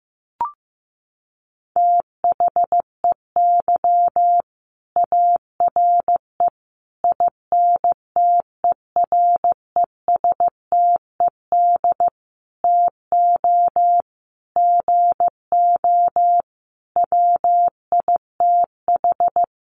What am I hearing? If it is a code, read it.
Morse code, heard as a keyed tone, about 15 wpm, 700 Hz. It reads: THEY ARE INTERESTED TO GO WITH
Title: Sentences from Top 900 Words 15wpm